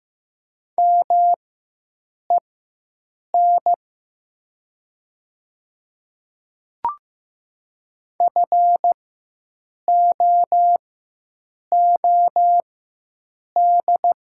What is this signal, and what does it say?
Morse code, heard as a keyed tone, about 15 wpm, spaced as for 7 wpm, 700 Hz. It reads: MEN FOOD
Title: 4x Character Spacing - Top 300 Words - 15wpm